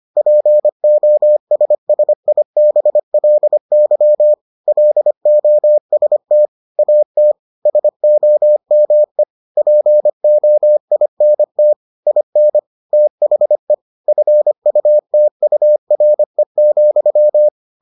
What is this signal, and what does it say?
Morse code, heard as a keyed tone, about 25 wpm, 600 Hz. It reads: POSSIBLY LOST AT SOME POINT IN THE FUTURE,